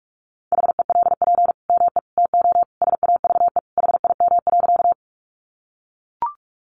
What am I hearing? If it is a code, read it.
Morse code, heard as a keyed tone, about 44 wpm, 700 Hz. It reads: HELP ME TO SAVE HIM.
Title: Sentences from Top 1100 Words 45wpm